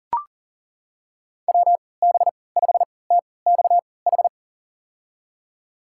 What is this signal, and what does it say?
Morse code, heard as a keyed tone, about 40 wpm, spaced as for 23 wpm, 700 Hz. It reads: WB5TXH
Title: USA Call Signs - 6 Characters - 3x Character Spacing - 40wpm